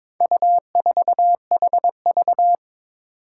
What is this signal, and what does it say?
Morse code, heard as a keyed tone, about 22 wpm, 700 Hz. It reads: U4HV
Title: Four Challenging Letter-Number Combinations 22wpm